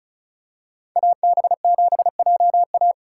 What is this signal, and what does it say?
Morse code, heard as a keyed tone, about 36 wpm, 700 Hz. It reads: AB7JA
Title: USA Call Signs - 3x Character Spacing - 35wpm